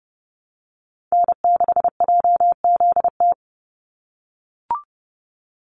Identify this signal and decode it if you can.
Morse code, heard as a keyed tone, about 30 wpm, 700 Hz. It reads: N6JZT
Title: NA SOTA Activator Callsigns 30wpm